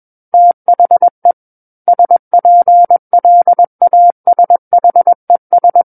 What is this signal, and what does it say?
Morse code, heard as a keyed tone, about 21 wpm, 700 Hz. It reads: THE SPLASHES
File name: Book One / 12-Chapter 12 - 21 WPM